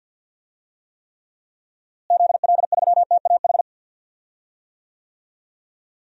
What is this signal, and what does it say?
Morse code, heard as a keyed tone, about 52 wpm, 700 Hz. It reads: ZL3TAH
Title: DXCC-100 Call Signs - 6 Characters - Repeated 3x - 50wpm